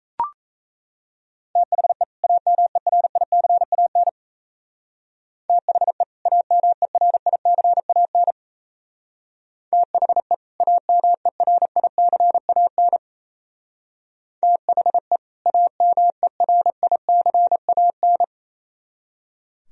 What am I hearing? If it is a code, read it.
Morse code, heard as a keyed tone, about 37 wpm, 700 Hz. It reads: THE AMERICAN THE AMERICAN THE AMERICAN THE AMERICAN